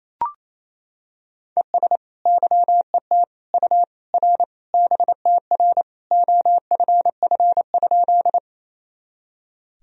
Morse code, heard as a keyed tone, about 28 wpm, 700 Hz. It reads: ES YET U R BTR OFF?
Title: Sentences from Top 500 Words and Abbreviations 28wpm